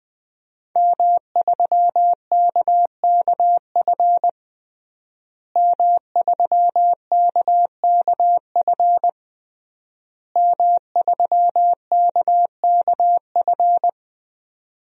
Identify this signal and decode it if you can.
Morse code, heard as a keyed tone, about 20 wpm, 700 Hz. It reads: M3KKF M3KKF M3KKF
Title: DXCC-100 Call Signs - 5 Characters - Repeated 3x - 20wpm